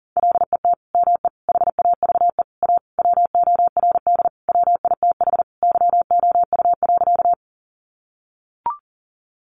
Morse code, heard as a keyed tone, about 40 wpm, 700 Hz. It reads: LET ME HAVE A WORD WITH YOU.
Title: Sentences from Top 300 Words 40wpm